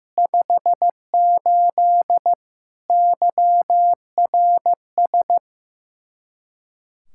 Morse code, heard as a keyed tone, about 15 wpm, 700 Hz. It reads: 58 YRS
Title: Practice QSO - Exchange 3 - More Info - 15wpm